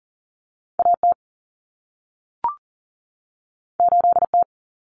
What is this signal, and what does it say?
Morse code, heard as a keyed tone, about 40 wpm, 700 Hz. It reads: AT 8T
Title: Common Cut Numbers - 2-Digit 40wpm